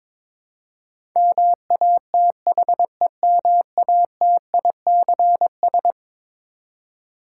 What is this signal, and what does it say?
Morse code, heard as a keyed tone, about 21 wpm, 700 Hz. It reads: MATHEMATICS
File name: practice / 22wpm / Eleven Letter Words - 22wpm